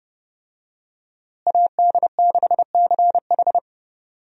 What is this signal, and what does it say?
Morse code, heard as a keyed tone, about 30 wpm, 700 Hz. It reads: AD6CH
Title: USA Call Signs - 5 Characters - Repeated 2x - 30wpm